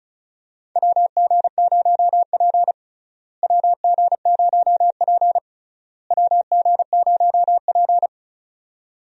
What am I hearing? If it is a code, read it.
Morse code, heard as a keyed tone, about 36 wpm, 700 Hz. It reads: WG0P WG0P WG0P
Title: USA Call Signs - 4 Characters - Repeated 3x - 35wpm